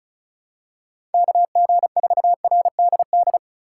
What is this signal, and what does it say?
Morse code, heard as a keyed tone, about 34 wpm, 690 Hz. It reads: KG4RDD
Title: USA Call Signs - 6 Characters - 3x Character Spacing - 35wpm